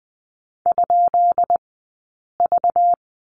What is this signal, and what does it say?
Morse code, heard as a keyed tone, about 20 wpm, spaced as for 8 wpm, 700 Hz. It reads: ?V
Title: 197 Lesson 32d - Letter k - Two Characters 20wpm